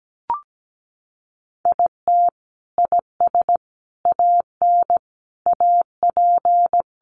Morse code, heard as a keyed tone, about 17 wpm, 700 Hz. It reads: IT IS AN AP
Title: Sentences from Top 1300 Words 17wpm